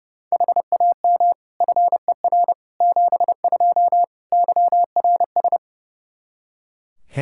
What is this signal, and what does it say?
Morse code, heard as a keyed tone, about 30 wpm, 700 Hz. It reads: HAM FER 72 YRS
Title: Practice QSO - Exchange 3 - More Info - 30wpm